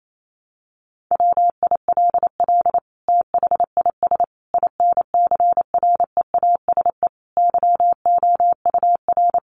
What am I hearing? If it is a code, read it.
Morse code, heard as a keyed tone, about 28 wpm, 700 Hz. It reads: WILL THIS INCREASE YOUR
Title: Sentences from Top 1200 Words 28wpm